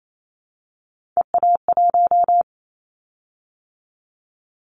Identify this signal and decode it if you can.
Morse code, heard as a keyed tone, about 28 wpm, 700 Hz. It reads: EA1
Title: Callsign Prefixes for Top 25 DXCC - 28wpm